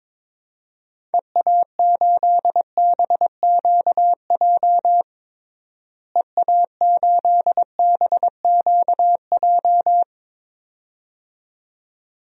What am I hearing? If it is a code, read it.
Morse code, heard as a keyed tone, about 22 wpm, 700 Hz. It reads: EA8BQJ EA8BQJ
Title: DXCC-100 Call Signs - 6 Characters - Repeated 2x - 22wpm